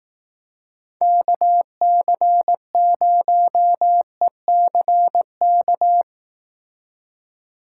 Morse code, heard as a keyed tone, about 19 wpm, 700 Hz. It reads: KC0ECK
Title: Speed Racing - USA 2x3 Call Signs 15wpm